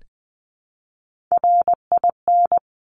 Morse code, heard as a keyed tone, about 20 wpm, 700 Hz. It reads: RIN
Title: Top Three Letter Combinations in Words 20wpm